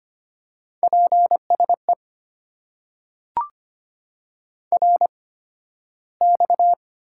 Morse code, heard as a keyed tone, about 25 wpm, spaced as for 13 wpm, 700 Hz. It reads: PSE RX
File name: Top 90 QSO Elements - 8x Character Spacing - 25wpm